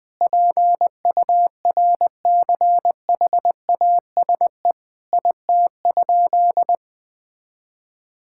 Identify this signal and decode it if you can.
Morse code, heard as a keyed tone, about 21 wpm, 700 Hz. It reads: PURCHASE IT?
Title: Sentences from Top 1000 Words - Review Part 5 - 20wpm